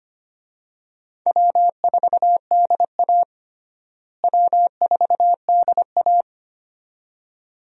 Morse code, heard as a keyed tone, about 25 wpm, 700 Hz. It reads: W4DA W4DA
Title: USA Call Signs - 4 Characters - Repeated 2x - 25wpm